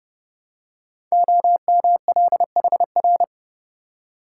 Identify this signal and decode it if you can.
Morse code, heard as a keyed tone, about 28 wpm, 700 Hz. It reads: OMLHR
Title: Code Groups - 5 Letters - 30wpm